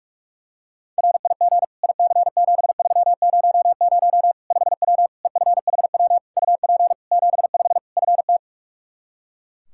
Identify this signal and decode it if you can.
Morse code, heard as a keyed tone, about 42 wpm, 700 Hz. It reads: RIG IC7300 5W EFHW UP 75 FT
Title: Practice QSO - Exchange 2 - Info - 45wpm